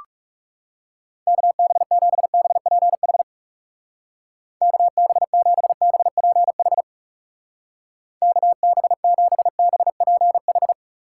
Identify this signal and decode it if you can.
Morse code, heard as a keyed tone, about 35 wpm, 700 Hz. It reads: KB7BPH KB7BPH KB7BPH
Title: Speed Racing - USA 2x3 Call Signs 30wpm